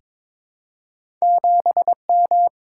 Morse code, heard as a keyed tone, about 22 wpm, 700 Hz. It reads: 7M
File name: practice / 22wpm / Two Letter-Number Combinations - 22wpm